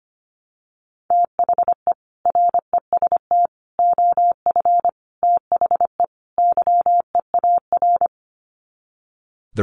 Morse code, heard as a keyed tone, about 25 wpm, 700 Hz. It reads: THE REST OF THE YEAR
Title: Sets of 5 Words Encore 25wpm